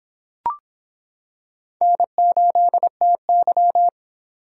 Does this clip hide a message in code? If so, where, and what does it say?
Yes: Morse code from 1.8 to 3.9 s, heard as a keyed tone, about 26 wpm, 700 Hz. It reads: N8TY